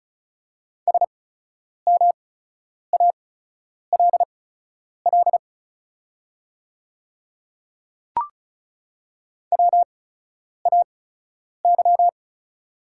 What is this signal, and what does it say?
Morse code, heard as a keyed tone, about 36 wpm, spaced as for 10 wpm, 690 Hz. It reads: SMALL WAY
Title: 8x Character Spacing - Top 200 Words - 35wpm